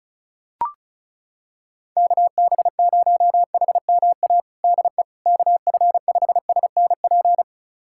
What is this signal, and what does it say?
Morse code, heard as a keyed tone, about 36 wpm, 700 Hz. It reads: KB0HMA DE KF5SNP
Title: USA Callsign DE Callsign - 6 Characters - 35wpm